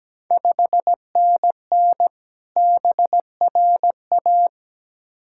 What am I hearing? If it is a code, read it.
Morse code, heard as a keyed tone, about 17 wpm, 700 Hz. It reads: 5NN BRA
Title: Ontario QSO Party Exchange 17wpm